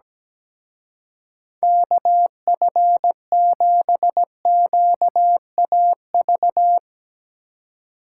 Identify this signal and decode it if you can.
Morse code, heard as a keyed tone, about 19 wpm, 700 Hz. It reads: KF7QAV
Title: Speed Racing - USA Call Signs 17wpm